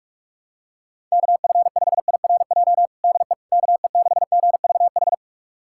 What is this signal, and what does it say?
Morse code, heard as a keyed tone, about 42 wpm, 700 Hz. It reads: KF5IRJ DE KE6GVH
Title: USA Callsign DE Callsign - 6 Characters - 45wpm